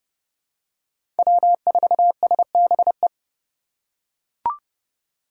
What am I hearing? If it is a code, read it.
Morse code, heard as a keyed tone, about 30 wpm, 690 Hz. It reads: W4SBE
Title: USA Call Signs - 5 Characters - 2x Character Spacing - 30wpm